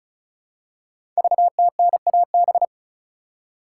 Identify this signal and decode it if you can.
Morse code, heard as a keyed tone, about 36 wpm, 700 Hz. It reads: VTNAB
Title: Code Groups - 5 Letters - 35wpm